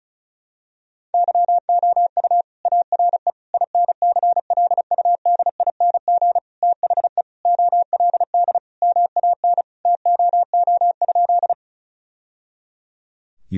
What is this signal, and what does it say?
Morse code, heard as a keyed tone, about 34 wpm, 700 Hz. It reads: YOU ARE INCLUDING THE OLD MAN TOO?
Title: Sentences from Top 1000 Words - Review Part 4 - 35wpm